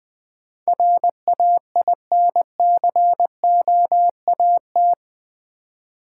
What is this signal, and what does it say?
Morse code, heard as a keyed tone, about 20 wpm, 700 Hz. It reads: RAINCOAT
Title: Easy Words with Hints - 20wpm